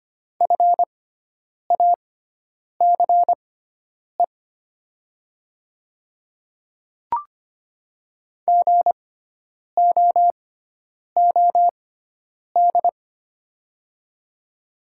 Morse code, heard as a keyed tone, about 25 wpm, spaced as for 9 wpm, 700 Hz. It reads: FACE GOOD
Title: 6x Character Spacing - Top 300 Words - 25wpm